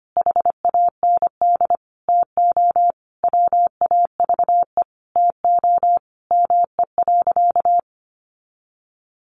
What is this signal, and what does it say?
Morse code, heard as a keyed tone, about 25 wpm, 700 Hz. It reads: HAND TO WAVE TO ME.
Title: Sentences from Top 1400 Words 25wpm